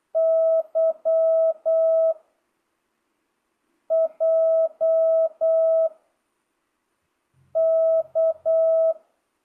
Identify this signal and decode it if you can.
Morse code, heard as a keyed tone, about 8 wpm, 660 Hz. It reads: Y J K
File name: jh8cbh / 2025cw